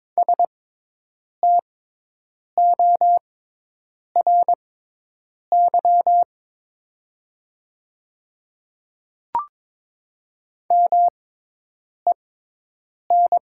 Morse code, heard as a keyed tone, about 22 wpm, spaced as for 8 wpm, 700 Hz. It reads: STORY MEN
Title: 6x Character Spacing - Top 500 Words - 22wpm